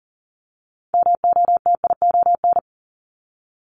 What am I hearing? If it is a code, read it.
Morse code, heard as a keyed tone, about 40 wpm, 700 Hz. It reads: MOTION
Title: Multi-letter Phonemes Repeated 3x 40wpm